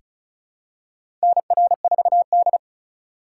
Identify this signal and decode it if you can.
Morse code, heard as a keyed tone, about 36 wpm, 700 Hz. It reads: NR4D
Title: USA Call Signs - 4 Characters - 2x Character Spacing - 35wpm